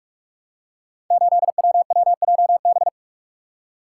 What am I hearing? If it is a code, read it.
Morse code, heard as a keyed tone, about 48 wpm, 700 Hz. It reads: 8WWJB
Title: Code Groups - 5 Letter:Numbers - 45wpm